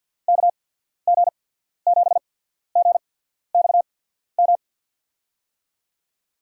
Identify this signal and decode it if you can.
Morse code, heard as a keyed tone, about 53 wpm, spaced as for 14 wpm, 700 Hz. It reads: KG7GXM
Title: USA Call Signs - 6 Characters - 8x Character Spacing - 50wpm